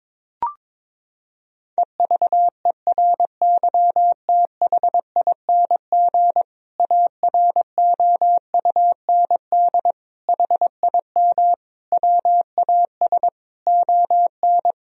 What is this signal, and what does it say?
Morse code, heard as a keyed tone, about 22 wpm, 700 Hz. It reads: EVERYTHING AROUND HIM WAS ON
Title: Sentences from Top 1000 Words - Review Part 3 - 22wpm